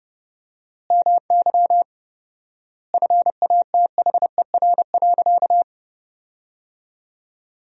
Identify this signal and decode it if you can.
Morse code, heard as a keyed tone, about 30 wpm, 700 Hz. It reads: MY FATHER.
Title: 4x Word-Spacing - Sentences from Top 300 Words - 30wpm